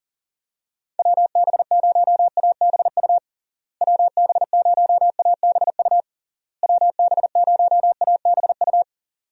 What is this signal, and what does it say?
Morse code, heard as a keyed tone, about 40 wpm, 700 Hz. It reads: WB0ABU WB0ABU WB0ABU
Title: USA Call Signs - 6 Characters - Repeated 3x - 40wpm